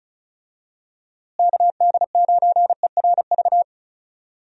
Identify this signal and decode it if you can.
Morse code, heard as a keyed tone, about 34 wpm, 700 Hz. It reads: KD9ERV
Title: USA Call Signs - 6 Characters - Repeated 3x - 35wpm